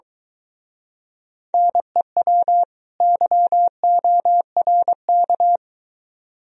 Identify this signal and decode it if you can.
Morse code, heard as a keyed tone, about 23 wpm, 700 Hz. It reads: NEW YORK